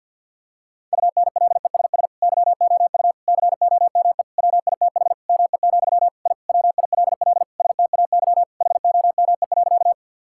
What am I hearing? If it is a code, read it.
Morse code, heard as a keyed tone, about 50 wpm, 700 Hz. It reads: UNLESS YOU COME WITH ME, I WILL STAY HOME.
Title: Sentences from Top 1000 Words - Review Part 6 - 50wpm